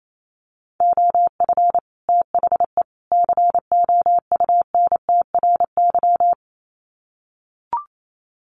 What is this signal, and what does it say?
Morse code, heard as a keyed tone, about 28 wpm, 700 Hz. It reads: OF THE COUNTRY
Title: Sets of 3 Words 28wpm